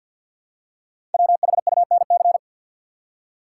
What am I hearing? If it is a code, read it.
Morse code, heard as a keyed tone, about 48 wpm, 700 Hz. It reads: WHUNC
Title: Code Groups - 5 Letters - 50wpm